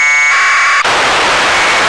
Packet radio (AX.25, as in APRS).
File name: lu4aao / globo19 1 LU2DPW